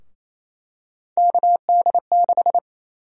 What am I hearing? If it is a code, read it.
Morse code, heard as a keyed tone, about 28 wpm, 700 Hz. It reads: KD6
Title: USA Call Sign Prefixes - Repeated 2x - 28wpm